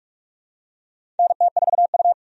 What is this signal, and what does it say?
Morse code, heard as a keyed tone, about 46 wpm, 700 Hz. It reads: NT4U
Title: USA Call Signs - 4 Characters - 45wpm